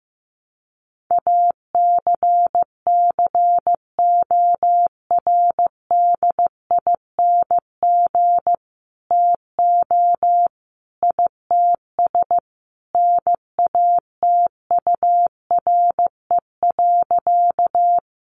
Morse code, heard as a keyed tone, about 15 wpm, 700 Hz. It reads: ACCORDING TO ITS NATURE.
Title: Sentences from Top 700 Words 15wpm